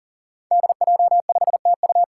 Morse code, heard as a keyed tone, about 38 wpm, 700 Hz. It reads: DJ5TU
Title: Five Challenging Letter-Number Combinations 40wpm